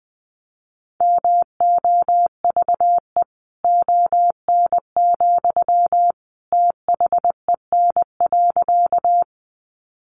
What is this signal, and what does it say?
Morse code, heard as a keyed tone, about 20 wpm, 700 Hz. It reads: MOVE ON, THEN.
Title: Sentences from Top 200 Words 20wpm